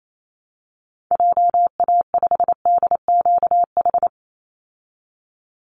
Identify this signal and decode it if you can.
Morse code, heard as a keyed tone, about 28 wpm, 690 Hz. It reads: JA5DQH
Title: Call Signs Repeated 3x - 28wpm